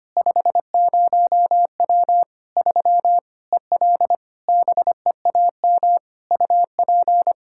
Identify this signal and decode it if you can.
Morse code, heard as a keyed tone, about 25 wpm, 700 Hz. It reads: 50W 3 EL BEAM UP
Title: Practice QSO - Exchange 2 - Info - 25wpm